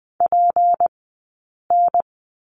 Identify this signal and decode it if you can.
Morse code, heard as a keyed tone, about 20 wpm, spaced as for 8 wpm, 700 Hz. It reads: PN
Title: 177 Lesson 29d - Slash - Two Characters 20wpm